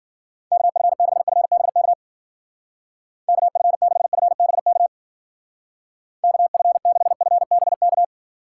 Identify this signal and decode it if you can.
Morse code, heard as a keyed tone, about 50 wpm, 700 Hz. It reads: KF6FBK KF6FBK KF6FBK
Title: Speed Racing - USA 2x3 Call Signs 40wpm